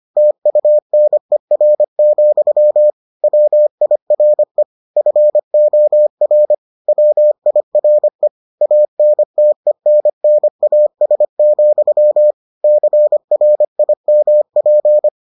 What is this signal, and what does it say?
Morse code, heard as a keyed tone, about 25 wpm, 600 Hz. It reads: TUNER, WIRE FOR WIRE ANTENNAS, CRIMP